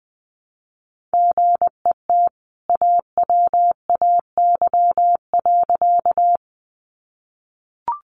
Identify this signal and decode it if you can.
Morse code, heard as a keyed tone, about 20 wpm, 700 Hz. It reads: GET AWAY.
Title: Sentences from Top 600 Words 20wpm